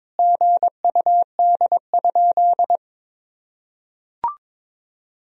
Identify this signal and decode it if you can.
Morse code, heard as a keyed tone, about 22 wpm, 700 Hz. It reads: GUD?
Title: Sentences from Top 200 Words and Abbreviations 22wpm